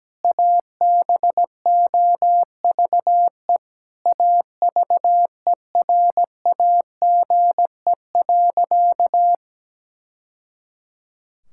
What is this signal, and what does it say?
Morse code, heard as a keyed tone, about 17 wpm, 700 Hz. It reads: ABOVE AVERAGE.
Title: Sentences from Top 1000 Words - Review Part 6 - 17wpm